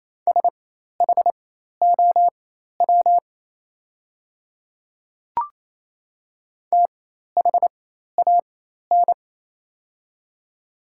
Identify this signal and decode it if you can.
Morse code, heard as a keyed tone, about 28 wpm, spaced as for 13 wpm, 700 Hz. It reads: SHOW THAN
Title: 4x Character Spacing - Top 300 Words - 28wpm